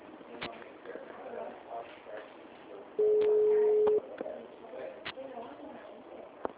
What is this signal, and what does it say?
Ringback tone (Europe, 425 Hz), with speech.